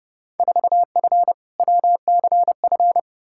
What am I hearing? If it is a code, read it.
Morse code, heard as a keyed tone, about 30 wpm, 700 Hz. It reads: 4F WCF
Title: Field Day Exchanges - 30wpm